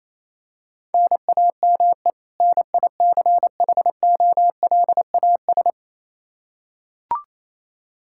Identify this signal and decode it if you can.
Morse code, heard as a keyed tone, about 28 wpm, 700 Hz. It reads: NAME NICHOLAS
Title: K1USN SST QSO Practice 28wpm